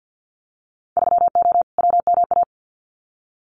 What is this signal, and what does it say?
Morse code, heard as a keyed tone, about 50 wpm, 700 Hz. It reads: 3O WMA
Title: Winter Field Day Exchanges Encore 50wpm